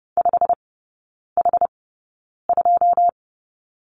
Morse code, heard as a keyed tone, about 30 wpm, spaced as for 9 wpm, 700 Hz. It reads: 5H2
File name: Lesson 18e - Letter m - Three Characters 30wpm